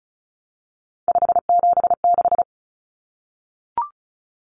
Morse code, heard as a keyed tone, about 34 wpm, 700 Hz. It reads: 576
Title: Common Cut Numbers - 3-Digit 35wpm